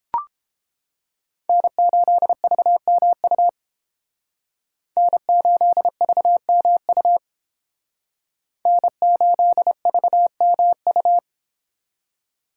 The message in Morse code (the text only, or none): N8VMU N8VMU N8VMU